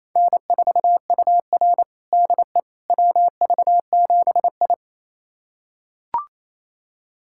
Morse code, heard as a keyed tone, about 28 wpm, 700 Hz. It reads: N4UR DE WV7I
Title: USA Callsign DE Callsign - 4 Characters - 28wpm